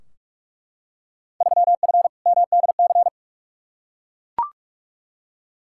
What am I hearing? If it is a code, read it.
Morse code, heard as a keyed tone, about 42 wpm, 700 Hz. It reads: 3F MDC
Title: Field Day Exchanges Encore - 45wpm